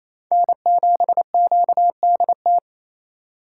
Morse code, heard as a keyed tone, about 36 wpm, 700 Hz. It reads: N7QDT